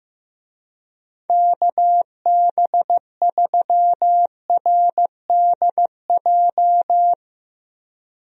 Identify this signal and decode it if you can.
Morse code, heard as a keyed tone, about 15 wpm, 700 Hz. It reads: KB3RDJ
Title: USA Call Signs - 6 Characters - Repeated 2x - 15wpm